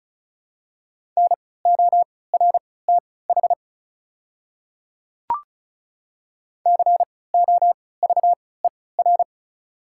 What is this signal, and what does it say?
Morse code, heard as a keyed tone, about 36 wpm, spaced as for 20 wpm, 700 Hz. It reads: NORTH COVER
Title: 3x Character Spacing - Top 500 Words - 35wpm